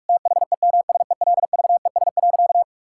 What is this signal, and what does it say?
Morse code, heard as a keyed tone, about 46 wpm, 700 Hz. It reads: THEMSELVES.
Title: Sentences from Top 1000 Words - Review Part 5 - 45wpm